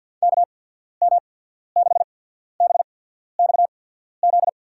Morse code, heard as a keyed tone, about 53 wpm, spaced as for 14 wpm, 700 Hz. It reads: KM6BXZ
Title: USA Call Signs - 6 Characters - 8x Character Spacing - 50wpm